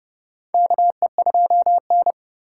Morse code, heard as a keyed tone, about 30 wpm, 700 Hz. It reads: KE2N
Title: USA Call Signs - 4 Characters - 4x Character Spacing - 30wpm